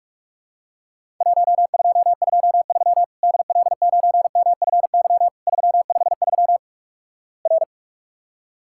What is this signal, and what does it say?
Morse code, heard as a keyed tone, about 47 wpm, 700 Hz. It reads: 1223 DL9MFY 353 R
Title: Worked All Europe DX Contest - QTC Exchange - 45wpm